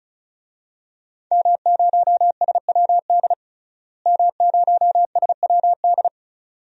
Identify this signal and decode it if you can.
Morse code, heard as a keyed tone, about 34 wpm, 700 Hz. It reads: M0SWD M0SWD
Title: DXCC-100 Call Signs - 5 Characters - Repeated 2x - 35wpm